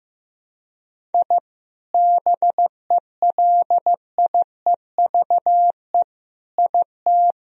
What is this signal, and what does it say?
Morse code, heard as a keyed tone, about 15 wpm, 700 Hz. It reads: I BELIEVE IT
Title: Sentences from Top 1000 Words - Review Part 1 - 15wpm